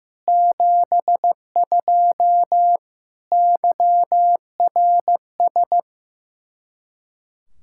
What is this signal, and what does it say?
Morse code, heard as a keyed tone, about 16 wpm, 700 Hz. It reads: 72 YRS
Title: Practice QSO - Exchange 3 - More Info - 15wpm